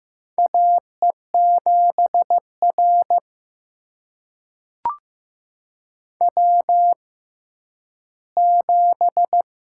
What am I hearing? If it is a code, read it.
Morse code, heard as a keyed tone, about 15 wpm, spaced as for 10 wpm, 700 Hz. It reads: AE7R W7
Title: USA Call Signs - 6x Character Spacing - 15wpm